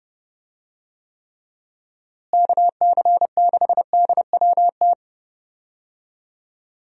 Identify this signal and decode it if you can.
Morse code, heard as a keyed tone, about 31 wpm, 700 Hz. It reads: KC6DWT
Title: USA Call Signs - 6 Characters - Repeated 2x - 30wpm